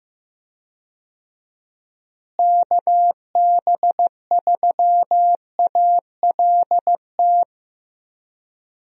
Morse code, heard as a keyed tone, about 15 wpm, 700 Hz. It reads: KB3ALT
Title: USA Call Signs - 6 Characters - Repeated 3x - 15wpm